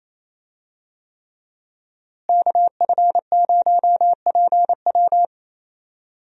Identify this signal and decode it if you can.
Morse code, heard as a keyed tone, about 28 wpm, 700 Hz. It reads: KF0PW
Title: USA Call Signs - 5 Characters - Repeated 3x - 28wpm